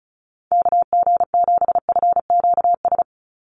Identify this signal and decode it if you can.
Morse code, heard as a keyed tone, about 34 wpm, 700 Hz. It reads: KG7FQS
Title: USA Call Signs 35wpm